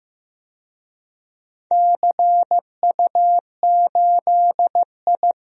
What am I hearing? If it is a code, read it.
Morse code, heard as a keyed tone, about 15 wpm, 700 Hz. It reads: CU8I
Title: DXCC-100 Call Signs - 4 Characters - Repeated 3x - 15wpm